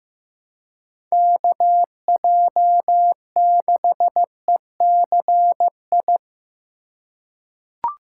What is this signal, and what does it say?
Morse code, heard as a keyed tone, about 16 wpm, 700 Hz. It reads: KJ6ECI
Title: USA Call Signs - 6 Characters - 3x Character Spacing - 15wpm